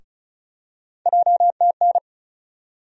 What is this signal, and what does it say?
Morse code, heard as a keyed tone, about 36 wpm, 700 Hz. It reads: JTN
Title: USA Call Sign Postfixes - 3 Letters 35wpm